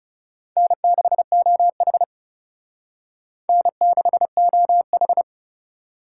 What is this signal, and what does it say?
Morse code, heard as a keyed tone, about 36 wpm, 700 Hz. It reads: N6OH N6OH
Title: Speed Racing - USA 1x2 Call Signs 30wpm